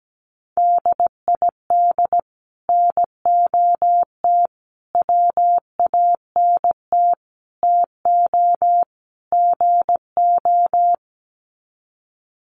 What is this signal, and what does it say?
Morse code, heard as a keyed tone, about 17 wpm, 700 Hz. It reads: DID NOT WANT TO GO
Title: Sets of 5 Words 17wpm